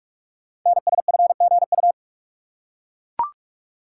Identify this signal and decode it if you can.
Morse code, heard as a keyed tone, about 44 wpm, 700 Hz. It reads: NSFGU
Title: Code Groups - 5 Letters - 45wpm